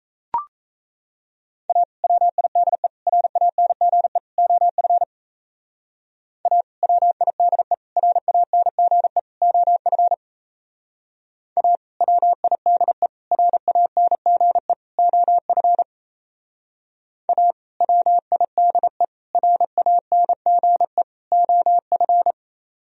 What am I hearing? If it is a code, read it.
Morse code, heard as a keyed tone, about 36 wpm, 700 Hz. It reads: A WIDE RANGE OF A WIDE RANGE OF A WIDE RANGE OF A WIDE RANGE OF